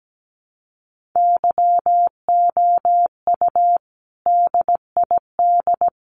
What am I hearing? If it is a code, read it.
Morse code, heard as a keyed tone, about 17 wpm, 700 Hz. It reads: YOU DID
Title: Sets of 5 Words Encore 17wpm